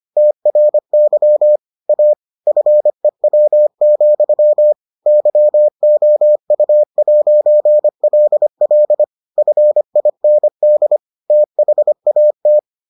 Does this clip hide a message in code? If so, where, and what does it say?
Yes: Morse code from 0.2 to 12.6 s, heard as a keyed tone, about 25 wpm, 600 Hz. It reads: TRY A FEW, YOU'LL FIND THAT